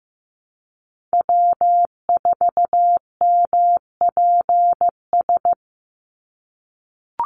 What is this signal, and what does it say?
Morse code, heard as a keyed tone, about 15 wpm, 700 Hz. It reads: W4MPS
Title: NA SOTA Chaser Callsigns 15wpm